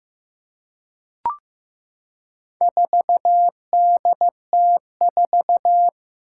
Morse code, heard as a keyed tone, about 15 wpm, 700 Hz. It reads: 4DT4
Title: Four Challenging Letter-Number Combinations 15wpm